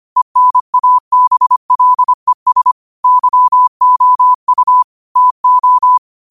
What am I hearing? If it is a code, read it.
Morse code, heard as a keyed tone, about 25 wpm, 1000 Hz. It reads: ENABLES YOU TO
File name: qotdfiles / 25wpm